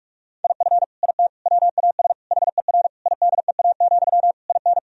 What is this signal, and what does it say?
Morse code, heard as a keyed tone, about 46 wpm, 700 Hz. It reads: IF IT WAS HER IDEA, IN
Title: Sentences from Top 1000 Words - Review Part 2 - 45wpm